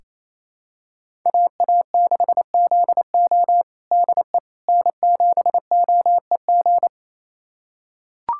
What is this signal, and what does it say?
Morse code, heard as a keyed tone, about 28 wpm, 700 Hz. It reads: AA6ZO DE N7OEG
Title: USA Callsign DE Callsign - 5 Characters - 28wpm